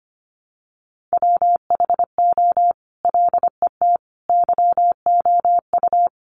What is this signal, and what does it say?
Morse code, heard as a keyed tone, about 25 wpm, 700 Hz. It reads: WHO LET YOU
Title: Sentences from Top 300 Words 25wpm